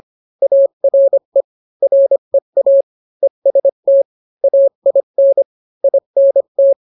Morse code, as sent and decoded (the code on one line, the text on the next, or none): .- .-. / .-. .- / . ... - / .- .. -. / .. -. -
ARE REA EST AIN INT